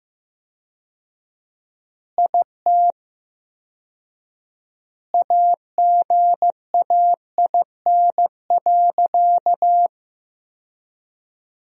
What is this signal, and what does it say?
Morse code, heard as a keyed tone, about 15 wpm, 700 Hz. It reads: IT AGAIN.
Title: 4x Word-Spacing - Sentences from Top 300 Words - 15wpm